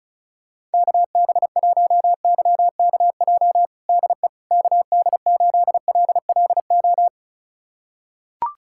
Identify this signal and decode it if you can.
Morse code, heard as a keyed tone, about 34 wpm, 700 Hz. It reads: KB1YKJ DE KD8LLO
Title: USA Callsign DE Callsign - 6 Characters - 35wpm